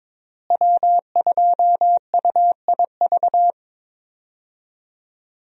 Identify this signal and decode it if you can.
Morse code, heard as a keyed tone, about 22 wpm, 700 Hz. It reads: W2UIV
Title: USA Call Signs - 5 Characters - Repeated 3x - 22wpm